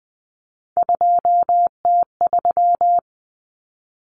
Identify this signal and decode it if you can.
Morse code, heard as a keyed tone, about 20 wpm, 700 Hz. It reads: 2T3
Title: Common Cut Numbers - 3-Digit 20wpm